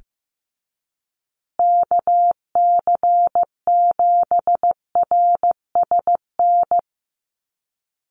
Morse code, heard as a keyed tone, about 15 wpm, 700 Hz. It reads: KC7RSN
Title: USA Call Signs Repeated 2x 15wpm